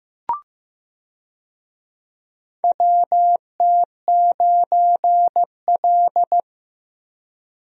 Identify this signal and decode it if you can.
Morse code, heard as a keyed tone, about 15 wpm, 700 Hz. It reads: WT9L
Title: USA Call Signs - 4 Characters - Repeated 3x - 15wpm